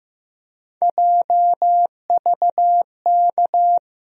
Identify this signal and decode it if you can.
Morse code, heard as a keyed tone, about 15 wpm, 700 Hz. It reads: JVK
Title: USA Call Sign Postfixes - 3 Letters 15wpm